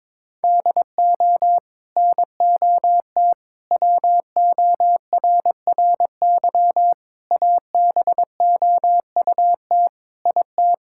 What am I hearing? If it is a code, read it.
Morse code, heard as a keyed tone, about 22 wpm, 700 Hz. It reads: DO NOT WORRY ABOUT IT
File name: practice / 22wpm / Build A Phrase - 5-Word Phrases - 22wpm